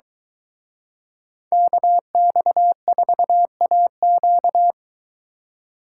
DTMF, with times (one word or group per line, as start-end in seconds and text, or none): none